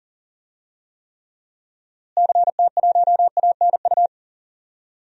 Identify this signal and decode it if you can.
Morse code, heard as a keyed tone, about 40 wpm, 700 Hz. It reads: CT1ANU
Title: DXCC-100 Call Signs - 6 Characters - Repeated 3x - 40wpm